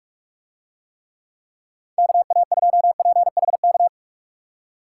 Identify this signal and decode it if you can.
Morse code, heard as a keyed tone, about 46 wpm, 700 Hz. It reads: KA2PHK